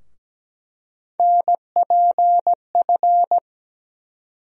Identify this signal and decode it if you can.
Morse code, heard as a keyed tone, about 17 wpm, 700 Hz. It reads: NPF
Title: USA Call Sign Postfixes - 3 Letters 17wpm